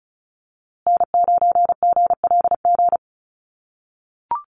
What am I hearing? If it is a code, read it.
Morse code, heard as a keyed tone, about 34 wpm, 700 Hz. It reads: N9GLG
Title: USA Call Signs Repeated 2x 35wpm